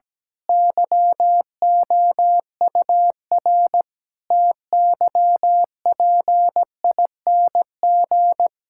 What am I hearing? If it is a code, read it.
Morse code, heard as a keyed tone, about 17 wpm, 700 Hz. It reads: YOUR TYPING
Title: Pangram Challenge v2 - 17wpm